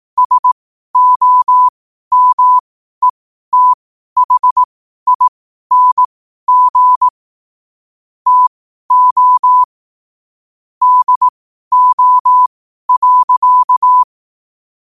Morse code, heard as a keyed tone, about 18 wpm, spaced as for 12 wpm, 1000 Hz. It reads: SOMETHING TO DO.
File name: 12wpm